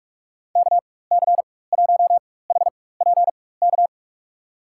Morse code, heard as a keyed tone, about 48 wpm, spaced as for 21 wpm, 700 Hz. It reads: KC1HPK
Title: USA Call Signs - 4x Character Spacing - 45wpm